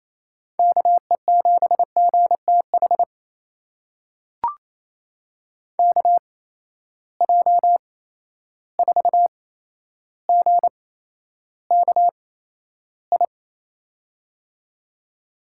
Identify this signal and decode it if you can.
Morse code, heard as a keyed tone, about 28 wpm, spaced as for 12 wpm, 700 Hz. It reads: KE7GTH KJ4GKI
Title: USA Call Signs - 8x Character Spacing - 28wpm